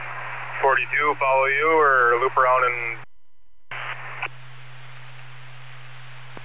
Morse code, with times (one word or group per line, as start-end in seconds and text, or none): none